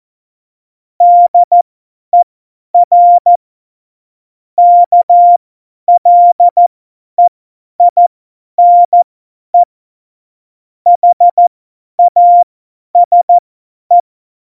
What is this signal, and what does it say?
Morse code, heard as a keyed tone, about 14 wpm, spaced as for 10 wpm, 700 Hz. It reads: DER KLEINE HASE
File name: Morsekurs / Witze / Witz 01. Mai 2018, 50 F70